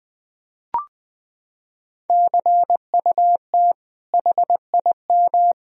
Morse code, heard as a keyed tone, about 20 wpm, 700 Hz. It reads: CUT HIM
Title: Sentences from Top 1000 Words - Review Part 1 - 20wpm